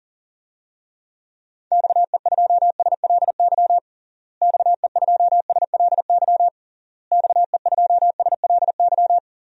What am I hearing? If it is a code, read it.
Morse code, heard as a keyed tone, about 40 wpm, 700 Hz. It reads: XE2SLY XE2SLY XE2SLY
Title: DXCC-100 Call Signs - 6 Characters - Repeated 3x - 40wpm